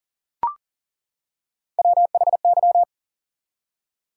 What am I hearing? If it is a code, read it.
Morse code, heard as a keyed tone, about 40 wpm, 700 Hz. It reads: WHY
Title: Three Letter Words - 40wpm